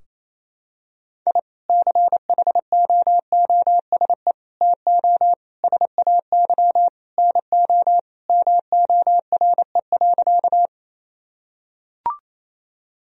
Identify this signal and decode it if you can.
Morse code, heard as a keyed tone, about 28 wpm, 700 Hz. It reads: I CHOOSE TO SAY NO MORE.
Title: Sentences from Top 1000 Words - Review Part 4 - 28wpm